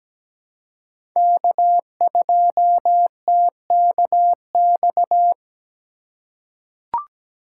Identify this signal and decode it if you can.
Morse code, heard as a keyed tone, about 17 wpm, 700 Hz. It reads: K2TKX
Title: USA Call Signs - 4x Character Spacing - 17wpm